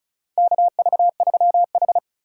Morse code, heard as a keyed tone, about 34 wpm, 700 Hz. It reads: KV3H
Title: USA Call Signs - 4 Characters - 4x Character Spacing - 35wpm